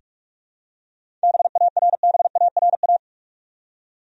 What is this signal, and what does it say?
Morse code, heard as a keyed tone, about 42 wpm, 700 Hz. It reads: BARBARA
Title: USA Common Names - 45wpm